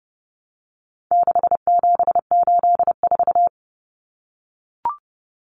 Morse code, heard as a keyed tone, about 30 wpm, 700 Hz. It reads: 6784
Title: Common Cut Numbers - 4-Digit 30wpm